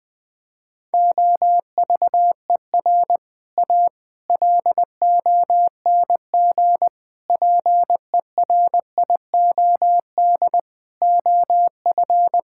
Morse code, heard as a keyed tone, about 20 wpm, 700 Hz. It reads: OVER A LONG PERIOD OF
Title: Sentences with 6-Word Phrases - 20wpm